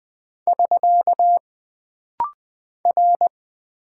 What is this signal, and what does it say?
Morse code, heard as a keyed tone, about 20 wpm, 700 Hz. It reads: <SK> R
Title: Single Character (Letter:Number:Punctuation:Prosign) - Rapid-Fire - 20wpm